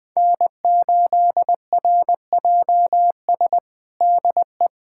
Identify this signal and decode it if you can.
Morse code, heard as a keyed tone, about 21 wpm, 700 Hz. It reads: N8RJS DE
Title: USA Callsign DE Callsign 20wpm